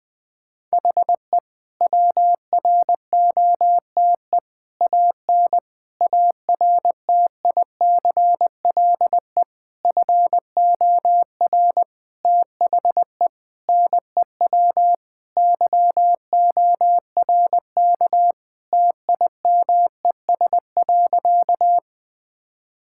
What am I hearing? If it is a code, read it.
Morse code, heard as a keyed tone, about 20 wpm, 700 Hz. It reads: HE WROTE AN ARTICLE FOR THE NEW YORK TIMES.
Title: Sentences with 5-Word Phrases - 20wpm